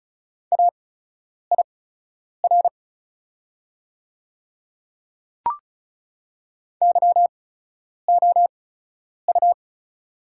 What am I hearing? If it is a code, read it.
Morse code, heard as a keyed tone, about 36 wpm, spaced as for 10 wpm, 700 Hz. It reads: AIR YOU
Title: 8x Character Spacing - Top 200 Words - 35wpm